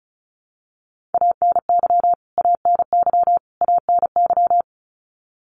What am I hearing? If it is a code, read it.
Morse code, heard as a keyed tone, about 34 wpm, 700 Hz. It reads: ANY ANY ANY
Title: Top 500 Words Repeated 35wpm